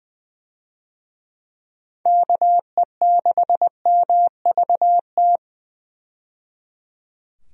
Morse code, heard as a keyed tone, about 20 wpm, 700 Hz. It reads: KE6MVT
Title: USA Call Signs - 6 Characters - 20wpm